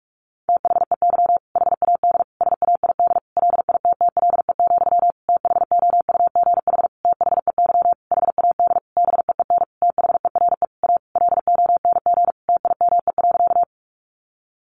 Morse code, heard as a keyed tone, about 44 wpm, 700 Hz. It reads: THEY HAD SAID LITTLE, THOUGH THEY HAD BEEN THERE A LONG TIME.
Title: Sentences from Top 400 Words 45wpm